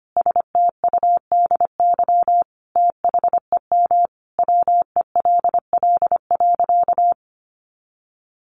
Morse code, heard as a keyed tone, about 25 wpm, 700 Hz. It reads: STUDY THEM WELL.
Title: Sentences from Top 300 Words 25wpm